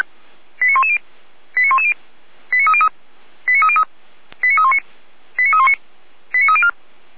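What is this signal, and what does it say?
Five-tone selective call (ZVEI), heard as quick runs of five tones, tones of about 70 ms. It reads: 89110 89110 89303 89303 89319 89319 89304
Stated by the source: Received on an SDR; ZVEI code 89305, from the Feuerwehr Orlamünde fire brigade server.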